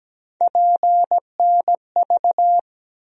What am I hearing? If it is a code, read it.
Morse code, heard as a keyed tone, about 17 wpm, 700 Hz. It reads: PNV
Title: USA Call Sign Postfixes - 3 Letters 17wpm